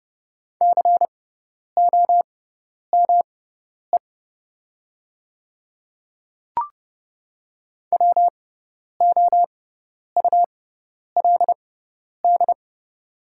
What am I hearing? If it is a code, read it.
Morse code, heard as a keyed tone, about 30 wpm, spaced as for 10 wpm, 690 Hz. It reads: COME WOULD
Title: 6x Character Spacing - Top 100 Words - 30wpm